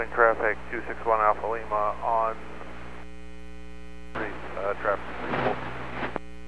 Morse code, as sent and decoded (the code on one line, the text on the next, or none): none